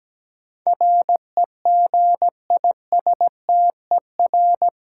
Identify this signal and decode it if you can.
Morse code, heard as a keyed tone, about 17 wpm, 700 Hz. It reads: REGISTER
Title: Sentences from Top 1200 Words - New Word Drill Alt - 17wpm